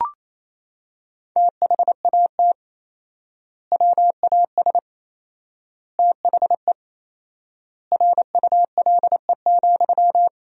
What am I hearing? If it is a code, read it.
Morse code, heard as a keyed tone, about 27 wpm, 700 Hz. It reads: THAT WAS THE RULE,
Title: 4x Word-Spacing - Sentences from Top 400 Words - 28wpm